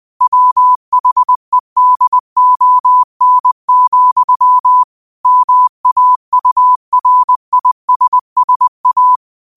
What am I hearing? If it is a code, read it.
Morse code, heard as a keyed tone, about 20 wpm, 1000 Hz. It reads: WHEDON, MAURISSA
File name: qotdfiles / 20wpm